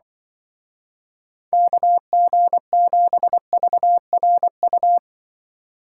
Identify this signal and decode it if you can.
Morse code, heard as a keyed tone, about 23 wpm, 700 Hz. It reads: KG7VRU
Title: Speed Racing - USA 2x3 Call Signs 20wpm